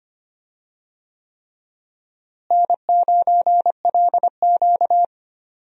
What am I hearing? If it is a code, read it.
Morse code, heard as a keyed tone, about 25 wpm, 700 Hz. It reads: N9LQ
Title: USA Call Signs - 4 Characters - Repeated 3x - 25wpm